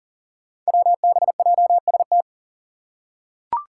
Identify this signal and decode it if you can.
Morse code, heard as a keyed tone, about 40 wpm, 700 Hz. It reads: WBJST
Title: Five Challenging Letter-Number Combinations 40wpm